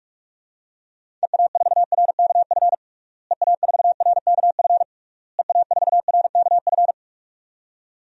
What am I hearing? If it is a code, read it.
Morse code, heard as a keyed tone, about 46 wpm, 700 Hz. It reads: EA4RKF EA4RKF EA4RKF
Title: DXCC-100 Call Signs - 6 Characters - Repeated 3x - 45wpm